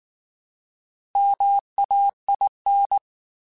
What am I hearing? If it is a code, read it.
Morse code, heard as a keyed tone, about 19 wpm, 800 Hz. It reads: MAIN
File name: Alphabetical / Alphabetical mp3s / Ms @18